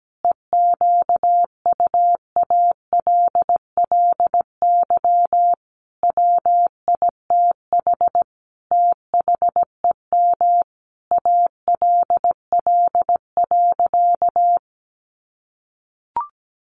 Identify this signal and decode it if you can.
Morse code, heard as a keyed tone, about 17 wpm, 700 Hz. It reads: EQUALLY WITH THEM ALL.
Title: Sentences from Top 1400 Words 17wpm